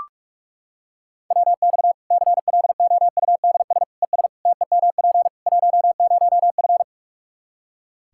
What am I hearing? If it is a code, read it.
Morse code, heard as a keyed tone, about 46 wpm, 700 Hz. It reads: WX CLOUDS ES TEMP 10F
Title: Practice QSO - Exchange 2 - Info - 45wpm